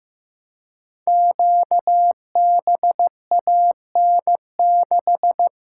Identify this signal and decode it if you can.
Morse code, heard as a keyed tone, about 15 wpm, 700 Hz. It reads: QBAN6
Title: Code Groups - 5 Letter:Numbers - 15wpm